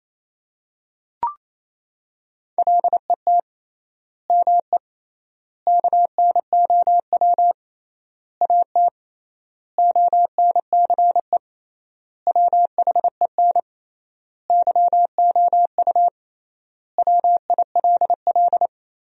Morse code, heard as a keyed tone, about 28 wpm, 700 Hz. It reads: LET ME KNOW AT ONCE WHEN YOU WILL
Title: 3x Word-Spacing - Sentences from Top 300 Words - 28wpm